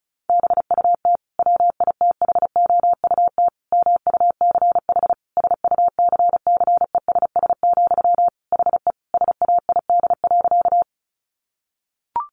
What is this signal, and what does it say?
Morse code, heard as a keyed tone, about 34 wpm, 700 Hz. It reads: BUT WITHOUT MUCH SUCCESS, HE SAID.
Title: Sentences from Top 900 Words 35wpm